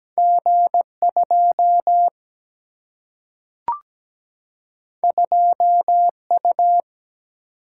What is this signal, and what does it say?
Morse code, heard as a keyed tone, about 17 wpm, 700 Hz. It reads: G2 2U
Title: Single-Letter Single-Number 17wpm